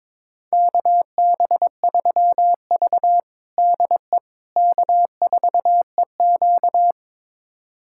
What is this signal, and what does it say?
Morse code, heard as a keyed tone, about 22 wpm, 700 Hz. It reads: KB3V DE K4EQ
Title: USA Callsign DE Callsign - 4 Characters - 22wpm